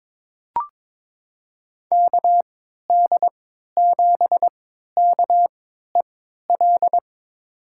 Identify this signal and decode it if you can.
Morse code, heard as a keyed tone, about 22 wpm, spaced as for 13 wpm, 700 Hz. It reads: KD7KEL
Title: USA Call Signs - 3x Character Spacing - 22wpm